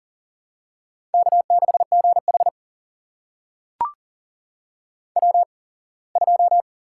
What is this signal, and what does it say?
Morse code, heard as a keyed tone, about 42 wpm, spaced as for 24 wpm, 700 Hz. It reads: K6GH W2
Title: USA Call Signs - 4 Characters - 8x Character Spacing - 40wpm